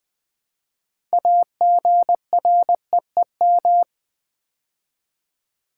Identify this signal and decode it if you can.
Morse code, heard as a keyed tone, about 20 wpm, 700 Hz. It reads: AGREEM
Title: Build A Word - 3 Letters At A Time - 9-Letter Words - 20wpm